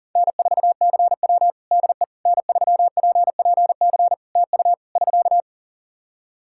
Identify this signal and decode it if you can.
Morse code, heard as a keyed tone, about 43 wpm, 700 Hz. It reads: N4CW DE N3PPC TU <SK>
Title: Practice QSO - Exchange 4 - Ending - 40wpm